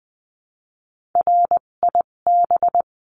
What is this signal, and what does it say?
Morse code, heard as a keyed tone, about 20 wpm, spaced as for 17 wpm, 700 Hz. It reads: RIB
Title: 257 Lesson 42b - Words 14-20wpm